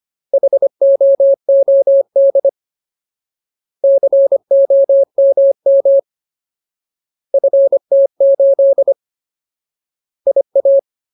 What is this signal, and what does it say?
Morse code, heard as a keyed tone, about 25 wpm, 550 Hz. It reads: HOOD COMM FT8 IA
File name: Ultimate Word List / 25WPM Ultimate